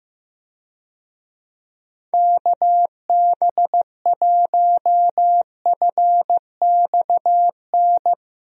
Morse code, heard as a keyed tone, about 15 wpm, 700 Hz. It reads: KB1FXN
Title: USA Call Signs - 6 Characters - Repeated 3x - 15wpm